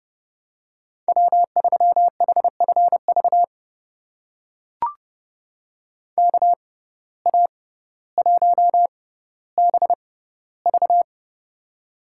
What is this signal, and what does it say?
Morse code, heard as a keyed tone, about 30 wpm, spaced as for 15 wpm, 700 Hz. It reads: W3HFV KA1BV
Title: USA Call Signs - 5 Characters - 6x Character Spacing - 30wpm